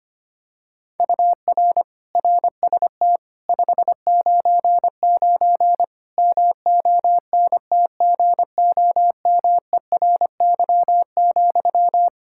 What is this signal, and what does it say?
Morse code, heard as a keyed tone, about 25 wpm, 700 Hz. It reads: UR RST 599 MONTGOMERY,
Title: K1USN SST QSO Practice 25wpm